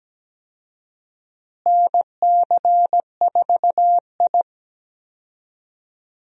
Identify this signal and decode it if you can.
Morse code, heard as a keyed tone, about 17 wpm, 700 Hz. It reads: NC4I
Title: USA Call Signs - 4 Characters - Repeated 2x - 17wpm